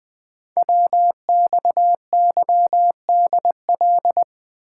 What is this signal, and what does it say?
Morse code, heard as a keyed tone, about 20 wpm, 700 Hz. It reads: WXYDL
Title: Code Groups - 5 Letters - 20wpm